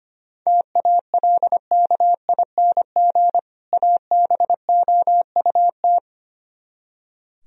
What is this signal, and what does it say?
Morse code, heard as a keyed tone, about 32 wpm, 700 Hz. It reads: TALKING ABOUT